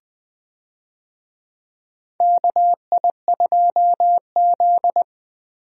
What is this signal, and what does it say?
Morse code, heard as a keyed tone, about 20 wpm, 700 Hz. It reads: KI2Z
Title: USA Call Signs - 4 Characters - 20wpm